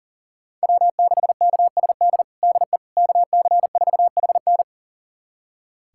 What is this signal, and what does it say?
Morse code, heard as a keyed tone, about 40 wpm, 700 Hz. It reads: W6KSD DE KC4HN
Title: USA Callsign DE Callsign - 5 Characters - 40wpm